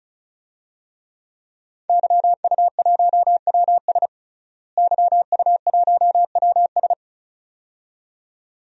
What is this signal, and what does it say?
Morse code, heard as a keyed tone, about 36 wpm, 700 Hz. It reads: YU1WS YU1WS
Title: DXCC-100 Call Signs - 5 Characters - Repeated 2x - 35wpm